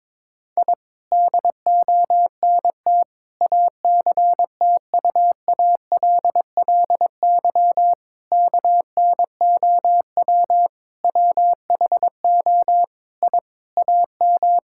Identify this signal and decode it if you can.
Morse code, heard as a keyed tone, about 22 wpm, 700 Hz. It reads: I DONT ACTUALLY KNOW WHO I AM
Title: Sentences from Top 1000 Words - Review Part 5 - 22wpm